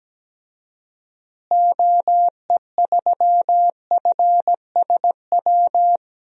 Morse code, heard as a keyed tone, about 17 wpm, 700 Hz. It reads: OE3FSW
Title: DXCC-100 Call Signs - 6 Characters - Repeated 3x - 17wpm